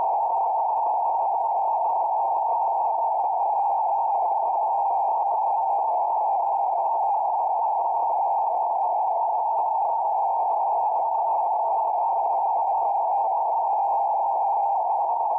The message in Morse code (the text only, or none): none